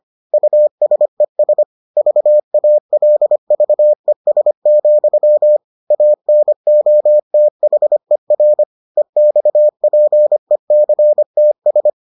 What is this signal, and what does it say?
Morse code, heard as a keyed tone, about 25 wpm, 600 Hz. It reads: USES VALVES, ANOTHER EXPECTS